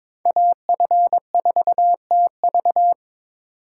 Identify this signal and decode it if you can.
Morse code, heard as a keyed tone, about 25 wpm, 700 Hz. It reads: AF4TV
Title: Speed Racing - USA 2x2 Call Signs 22wpm